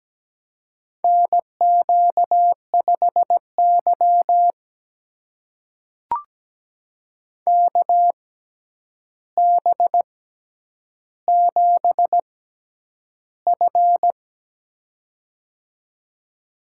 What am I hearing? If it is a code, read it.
Morse code, heard as a keyed tone, about 17 wpm, spaced as for 9 wpm, 700 Hz. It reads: NQ5Y KB7F
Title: USA Call Signs - 4 Characters - 6x Character Spacing - 17wpm